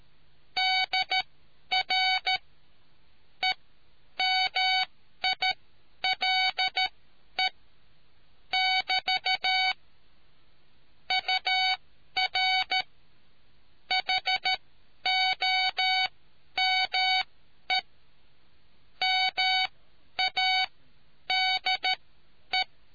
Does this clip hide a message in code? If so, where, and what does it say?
Yes: Morse code from 0.6 to 22.6 s, heard as a keyed tone, about 13 wpm, spaced as for 10 wpm, 750 Hz. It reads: DR EMILE <BT> UR HOME MADE